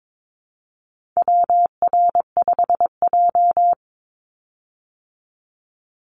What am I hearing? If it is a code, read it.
Morse code, heard as a keyed tone, about 22 wpm, 700 Hz. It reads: WR5J
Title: Call Signs Repeated 2x - 22wpm